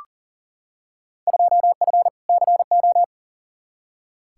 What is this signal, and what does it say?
Morse code, heard as a keyed tone, about 38 wpm, 700 Hz. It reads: 2F CO
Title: Field Day Exchanges - 40wpm